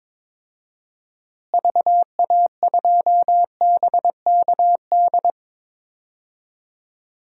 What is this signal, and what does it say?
Morse code, heard as a keyed tone, about 22 wpm, 700 Hz. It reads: VA2BKD
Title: DXCC-100 Call Signs - 6 Characters - Repeated 2x - 22wpm